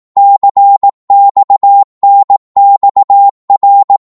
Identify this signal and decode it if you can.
Morse code, heard as a keyed tone, about 18 wpm, 800 Hz. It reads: CXNXR